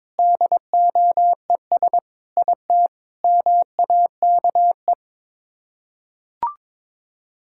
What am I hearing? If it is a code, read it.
Morse code, heard as a keyed tone, about 22 wpm, 700 Hz. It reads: DOES IT MAKE
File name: practice / 22wpm / Sets of 8 Words - 22wpm